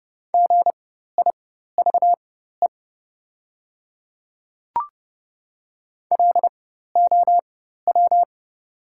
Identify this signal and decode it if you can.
Morse code, heard as a keyed tone, about 30 wpm, spaced as for 14 wpm, 700 Hz. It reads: GIVE LOW
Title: 4x Character Spacing - Top 200 Words - 30wpm